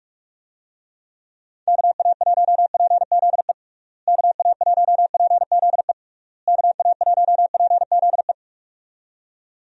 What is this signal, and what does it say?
Morse code, heard as a keyed tone, about 43 wpm, 700 Hz. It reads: KA1PZE KA1PZE KA1PZE
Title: USA Call Signs - 6 Characters - Repeated 3x - 45wpm